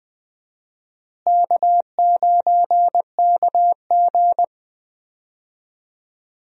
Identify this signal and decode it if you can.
Morse code, heard as a keyed tone, about 20 wpm, 700 Hz. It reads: K9KG
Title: USA Call Signs - 4 Characters - Repeated 3x - 20wpm